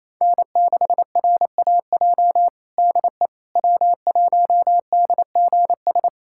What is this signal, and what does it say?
Morse code, heard as a keyed tone, about 28 wpm, 700 Hz. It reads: N6RAJ DE W1DGS
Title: USA Callsign DE Callsign - 5 Characters - 28wpm